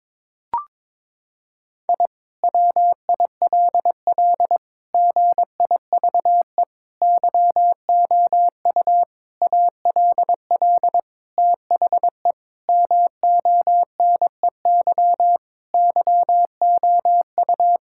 Morse code, heard as a keyed tone, about 22 wpm, 700 Hz. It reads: I WILL GIVE YOU ALL THE MONEY YOU
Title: Sentences from Top 1000 Words - Review Part 6 - 22wpm